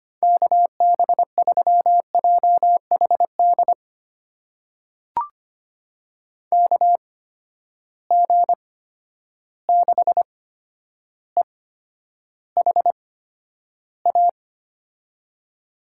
Morse code, heard as a keyed tone, about 25 wpm, spaced as for 11 wpm, 700 Hz. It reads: KB3JHD KG6EHA
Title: USA Call Signs - 6 Characters - 8x Character Spacing - 25wpm